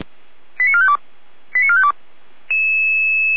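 Five-tone selective call (ZVEI), heard as quick runs of five tones, tones of about 70 ms. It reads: 89452 89452
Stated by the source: ZVEI five-tone selective call received on an SDR.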